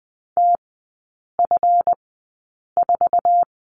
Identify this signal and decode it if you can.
Morse code, heard as a keyed tone, about 20 wpm, spaced as for 8 wpm, 700 Hz. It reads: TF4
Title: 166 Lesson 27e - Number 7 - Three Characters 20wpm